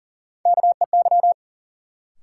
Morse code, heard as a keyed tone, about 40 wpm, 700 Hz. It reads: KEY
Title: Sentences from Top 600 Words - New Word Drill - 40wpm